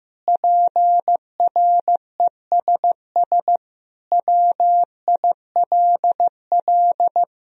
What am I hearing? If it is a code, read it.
Morse code, heard as a keyed tone, about 16 wpm, 700 Hz. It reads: PRESS WILL
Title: Sentences from Top 1000 Words - Review Part 1 - 15wpm